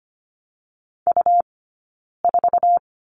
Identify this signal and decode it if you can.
Morse code, heard as a keyed tone, about 25 wpm, 700 Hz. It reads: U 4
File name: course / 25:10wpm / 099 Lesson 17d - Letter u - Two Characters 25wpm